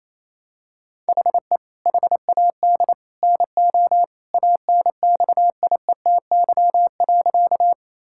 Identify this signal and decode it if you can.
Morse code, heard as a keyed tone, about 28 wpm, 700 Hz. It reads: HE HAD NO ANXIETY.
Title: Sentences from Top 1600 Words 28wpm